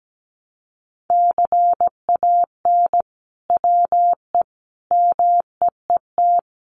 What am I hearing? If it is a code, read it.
Morse code, heard as a keyed tone, about 17 wpm, 700 Hz. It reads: CAN WE MEET
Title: Sentences from Top 1100 Words 17wpm